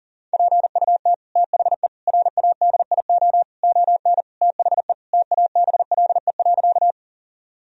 Morse code, heard as a keyed tone, about 40 wpm, 700 Hz. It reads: PUT THE RADIO ON THE TABLE.
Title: Sentences from Top 1000 Words - Review Part 3 - 40wpm